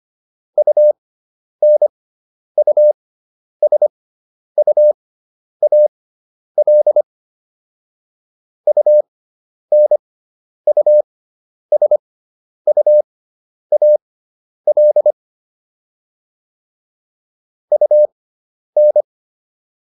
Morse code, heard as a keyed tone, about 25 wpm, spaced as for 10 wpm, 600 Hz. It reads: UNUSUAL UNUSUAL UN